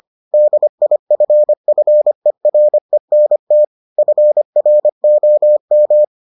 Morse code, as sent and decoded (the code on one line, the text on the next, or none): -.. .. ..-. ..-. .-. -. - / ..-. .-. --- --
DIFFERENT FROM